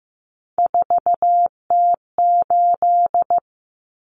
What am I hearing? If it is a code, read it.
Morse code, heard as a keyed tone, about 15 wpm, 700 Hz. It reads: 4T8
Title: Common Cut Numbers - 3-Digit 15wpm